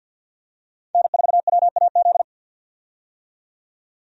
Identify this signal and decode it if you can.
Morse code, heard as a keyed tone, about 46 wpm, 700 Hz. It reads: N4WAZ
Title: USA Call Signs - 5 Characters - 50wpm